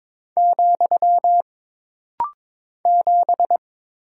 Morse code, heard as a keyed tone, about 22 wpm, 700 Hz. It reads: , 7
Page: This rapid-fire timing is geared to push your ICR (Instant Character Recognition) skills.